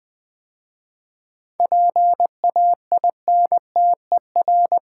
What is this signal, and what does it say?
Morse code, heard as a keyed tone, about 21 wpm, 700 Hz. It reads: PAINTER
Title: 7 Letter Words - First 3 Letters Given - 20wpm